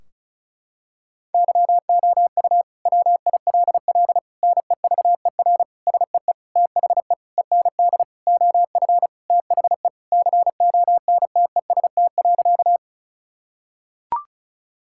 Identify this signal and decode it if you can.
Morse code, heard as a keyed tone, about 36 wpm, 700 Hz. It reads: YOU WILL NEVER SEE THE END OF THE CONTEST.
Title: Sentences from Top 1600 Words 35wpm